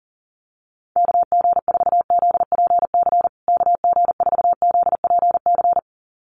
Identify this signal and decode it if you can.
Morse code, heard as a keyed tone, about 40 wpm, 700 Hz. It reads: KG4ZPC KG4ZPC
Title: USA Call Signs Repeated 2x 40wpm